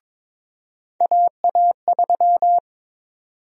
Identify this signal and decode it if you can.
Morse code, heard as a keyed tone, about 22 wpm, 700 Hz. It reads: AA3
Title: USA Call Sign Prefixes - 22wpm